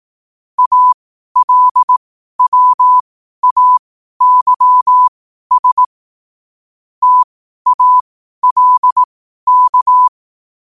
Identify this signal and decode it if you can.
Morse code, heard as a keyed tone, about 18 wpm, spaced as for 12 wpm, 1000 Hz. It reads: ALWAYS TALK
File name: qotdfiles / 12wpm